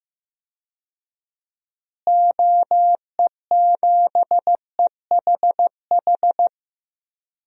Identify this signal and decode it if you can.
Morse code, heard as a keyed tone, about 15 wpm, 700 Hz. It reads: OE7EHH
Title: DXCC-100 Call Signs - 6 Characters - Repeated 3x - 15wpm